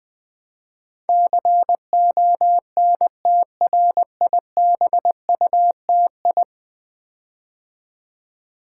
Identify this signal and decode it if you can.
Morse code, heard as a keyed tone, about 21 wpm, 700 Hz. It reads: CONTRIBUTI
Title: Build A Word - 5 Letters At A Time - 12-Letter Words - 20wpm